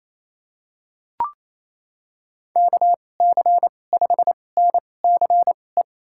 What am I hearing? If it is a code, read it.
Morse code, heard as a keyed tone, about 28 wpm, spaced as for 20 wpm, 700 Hz. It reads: KC5NCE
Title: USA Call Signs - 6 Characters - 2x Character Spacing - 28wpm